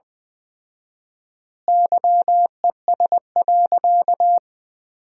Morse code, heard as a keyed tone, about 21 wpm, 700 Hz. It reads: YES.
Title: 4x Word-Spacing - Sentences from Top 500 Words - 20wpm